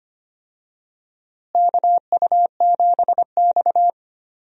Morse code, heard as a keyed tone, about 25 wpm, 700 Hz. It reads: KU7X
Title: USA Call Signs - 4 Characters - Repeated 2x - 25wpm